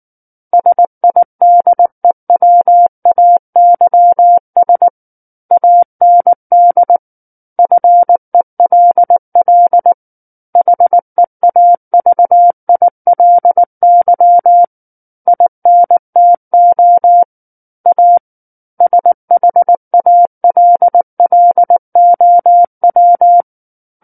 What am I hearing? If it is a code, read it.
Morse code, heard as a keyed tone, about 19 wpm, 700 Hz. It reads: SIDEWAYS AND FELL HEAVILY INTO A SHALLOW
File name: Book One / 10-Chapter 10 - 19 WPM